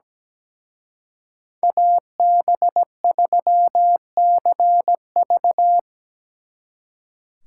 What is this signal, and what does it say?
Morse code, heard as a keyed tone, about 20 wpm, 700 Hz. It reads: AB3CV
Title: Speed Racing - USA 2x2 Call Signs 17wpm